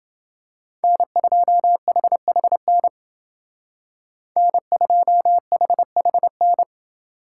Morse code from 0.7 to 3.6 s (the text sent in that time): N2HHN